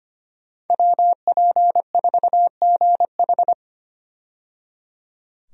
Morse code, heard as a keyed tone, about 25 wpm, 700 Hz. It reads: WP4GH
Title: USA Call Signs - 5 Characters - 25wpm